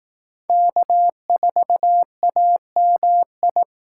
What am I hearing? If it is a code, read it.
Morse code, heard as a keyed tone, about 18 wpm, 700 Hz. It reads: K4AMI
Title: Speed Racing - USA 1x3 Call Signs 15wpm